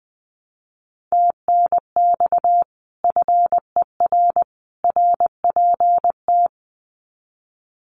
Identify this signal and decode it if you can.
Morse code, heard as a keyed tone, about 20 wpm, 700 Hz. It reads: TNX FER RPT
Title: Common QSO Phrases 20wpm